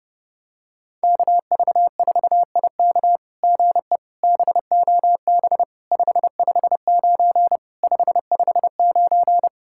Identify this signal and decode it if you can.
Morse code, heard as a keyed tone, about 30 wpm, 700 Hz. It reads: KV4IK GE BOB 559 559
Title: SOTA QSO Practice 30wpm